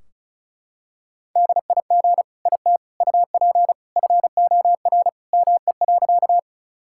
Morse code, heard as a keyed tone, about 34 wpm, 700 Hz. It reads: DIG IT UP FOR ME.
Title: Sentences from Top 1600 Words 35wpm